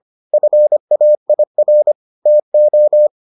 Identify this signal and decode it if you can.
Morse code, heard as a keyed tone, about 25 wpm, 600 Hz. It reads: FAIR TO